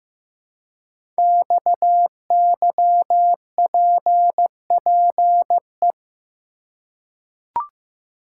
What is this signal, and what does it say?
Morse code, heard as a keyed tone, about 15 wpm, 700 Hz. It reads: XYPPE
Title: Code Groups - 5 Letters - 15wpm